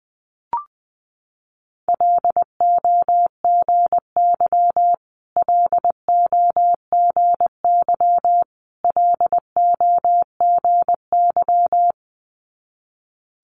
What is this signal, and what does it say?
Morse code, heard as a keyed tone, about 20 wpm, 700 Hz. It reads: LOGY LOGY LOGY
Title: Multi-letter Phonemes Repeated 3x 20wpm